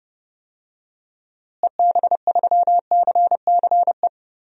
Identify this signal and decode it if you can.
Morse code, heard as a keyed tone, about 31 wpm, 700 Hz. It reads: EB3CCE
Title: DXCC-100 Call Signs - 6 Characters - Repeated 3x - 30wpm